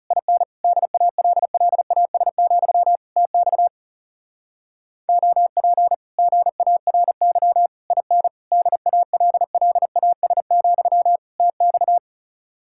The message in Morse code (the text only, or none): IN DALLAS, TX OP GARY IN DALLAS, TX